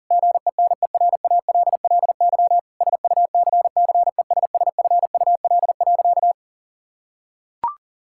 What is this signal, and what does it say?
Morse code, heard as a keyed tone, about 40 wpm, 700 Hz. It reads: GENERALLY SUCCESSFUL.
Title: Sentences from Top 1000 Words - Review Part 6 - 40wpm